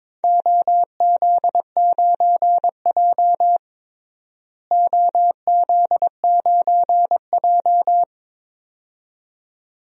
Morse code, heard as a keyed tone, about 22 wpm, 700 Hz. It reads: OZ9J OZ9J
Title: DXCC-100 Call Signs - 4 Characters - Repeated 2x - 22wpm